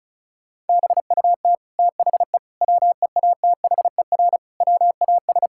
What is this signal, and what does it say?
Morse code, heard as a keyed tone, about 36 wpm, 700 Hz. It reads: BUT THE WEATHER WAS
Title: Sentences from Top 1000 Words - Review Part 5 - 35wpm